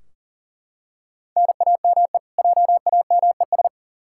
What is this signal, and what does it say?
Morse code, heard as a keyed tone, about 39 wpm, 700 Hz. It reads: NAME JAMES
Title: Practice QSO - Exchange 1 - Beginning - 40wpm